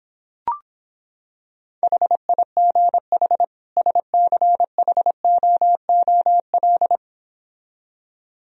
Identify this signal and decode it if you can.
Morse code, heard as a keyed tone, about 26 wpm, 700 Hz. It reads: HIGH SCHOOL